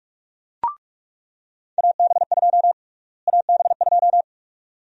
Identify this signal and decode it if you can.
Morse code, heard as a keyed tone, about 46 wpm, 700 Hz. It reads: AB2 AB2
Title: USA Call Sign Prefixes - Repeated 2x - 45wpm